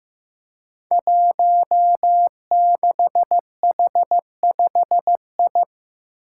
Morse code, heard as a keyed tone, about 15 wpm, 700 Hz. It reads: 16H5I
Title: Five Challenging Letter-Number Combinations 15wpm